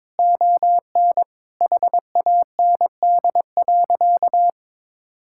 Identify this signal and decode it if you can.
Morse code, heard as a keyed tone, about 22 wpm, 700 Hz. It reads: ON HAND.
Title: Sentences from Top 1000 Words - Review Part 6 - 22wpm